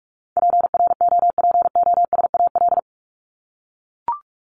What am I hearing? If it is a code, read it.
Morse code, heard as a keyed tone, about 44 wpm, 700 Hz. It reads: PROPOSAL
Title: Top 1751-2000 Words 45wpm